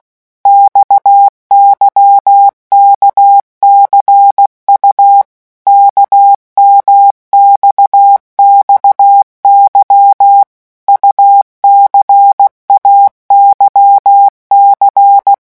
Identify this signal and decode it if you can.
Morse code, heard as a keyed tone, about 16 wpm, 800 Hz. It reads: XYKCU KMXXY UCAYC
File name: morse mp3 / revision2-vitesse 15 mots